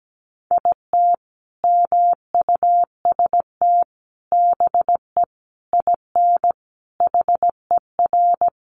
Morse code, heard as a keyed tone, about 17 wpm, 700 Hz. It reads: IT MUST BE IN HER
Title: Sentences from Top 300 Words 17wpm